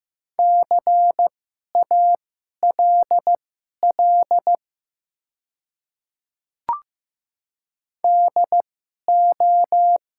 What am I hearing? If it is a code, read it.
Morse code, heard as a keyed tone, about 15 wpm, spaced as for 11 wpm, 700 Hz. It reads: CALL DO
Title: 2x Character Spacing - Top 100 Words - 15wpm